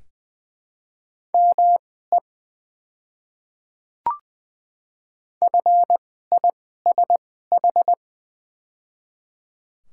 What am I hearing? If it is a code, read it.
Morse code, heard as a keyed tone, about 20 wpm, spaced as for 14 wpm, 700 Hz. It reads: ME FISH
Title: 2x Character Spacing - Top 300 Words - 20wpm